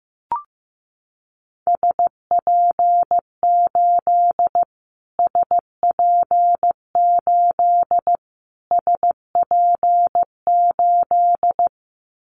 Morse code, heard as a keyed tone, about 15 wpm, 700 Hz. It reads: SP8 SP8 SP8
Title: Callsign Prefixes for Top 25 DXCC - Repeated 3x - 15wpm